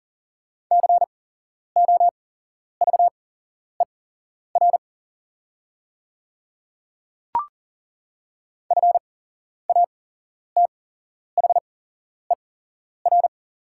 Morse code, heard as a keyed tone, about 40 wpm, spaced as for 11 wpm, 690 Hz. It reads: COVER FATHER
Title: 8x Character Spacing - Top 500 Words - 40wpm